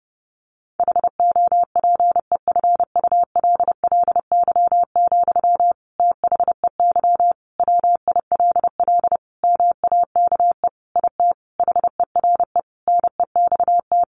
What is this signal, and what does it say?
Morse code, heard as a keyed tone, about 30 wpm, 700 Hz. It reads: HOPEFULLY, THEY WILL MAKE IT HERE NEXT
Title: Sentences from Top 1500 Words 30wpm